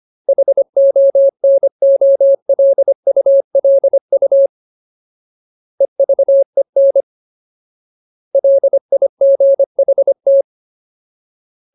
Morse code, heard as a keyed tone, about 25 wpm, 550 Hz. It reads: HONOLULU EVEN LIGHT
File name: Ultimate Word List / 25WPM Ultimate